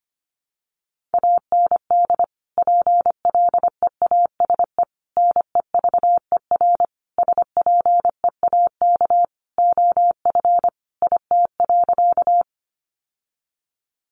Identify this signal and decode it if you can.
Morse code, heard as a keyed tone, about 25 wpm, 700 Hz. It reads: AND PLEASE NEVER SPEAK OF IT.
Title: Sentences from Top 1500 Words 25wpm